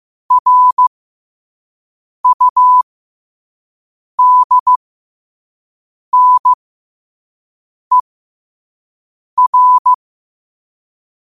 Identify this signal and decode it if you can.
Morse code, heard as a keyed tone, about 15 wpm, spaced as for 5 wpm, 1000 Hz. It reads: RUDNER
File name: qotdfiles / 5wpm